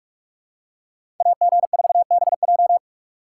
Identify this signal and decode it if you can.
Morse code, heard as a keyed tone, about 42 wpm, 700 Hz. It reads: AG4BJ